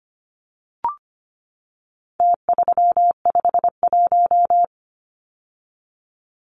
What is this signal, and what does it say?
Morse code, heard as a keyed tone, about 25 wpm, 700 Hz. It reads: T351
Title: Numbers - Contest Style - 4-Digit 25wpm